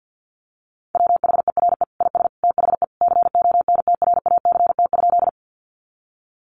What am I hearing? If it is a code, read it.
Morse code, heard as a keyed tone, about 50 wpm, 700 Hz. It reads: WHERE IS THE CONTRACT?
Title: Sentences from Top 1400 Words 50wpm